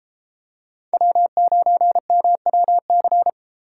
Morse code, heard as a keyed tone, about 33 wpm, 700 Hz. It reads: W9MWC